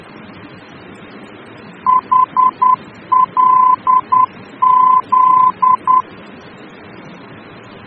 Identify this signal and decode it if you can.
Morse code, heard as a keyed tone, about 10 wpm, 1020 Hz. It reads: HLZ
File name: germany / mx / HLZ-D-VOR